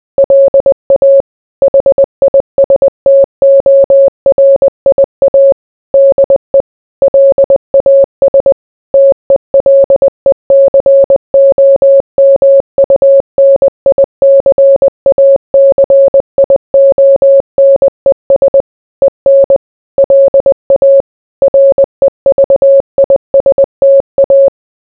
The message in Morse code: LA HISTORIA DE LAS TELECOMUNICACIONES EN LA REVISTA